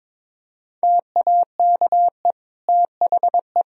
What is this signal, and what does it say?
Morse code, heard as a keyed tone, about 22 wpm, 700 Hz. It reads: TAKE THE
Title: Sentences from Top 1000 Words - Review Part 1 - 22wpm